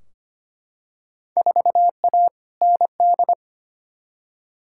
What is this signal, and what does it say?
Morse code, heard as a keyed tone, about 25 wpm, 700 Hz. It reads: 4A ND
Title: Field Day Exchanges Encore - 25wpm